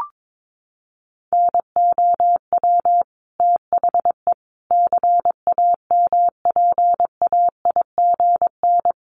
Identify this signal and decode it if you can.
Morse code, heard as a keyed tone, about 22 wpm, 700 Hz. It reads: NOW THE CAMPAIGN
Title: Sentences from Top 1200 Words 22wpm